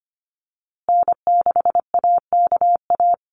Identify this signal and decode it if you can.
Morse code, heard as a keyed tone, about 25 wpm, 700 Hz. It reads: N6AKA
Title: USA Call Signs Repeated 2x 25wpm